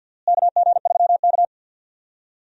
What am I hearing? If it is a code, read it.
Morse code, heard as a keyed tone, about 49 wpm, 700 Hz. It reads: KG3K
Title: USA Call Signs - 4 Characters - Repeated 2x - 50wpm